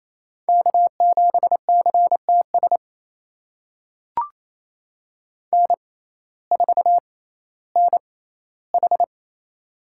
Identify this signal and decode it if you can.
Morse code, heard as a keyed tone, about 28 wpm, spaced as for 15 wpm, 700 Hz. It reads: K7CTS N4NH
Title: USA Call Signs - 6x Character Spacing - 28wpm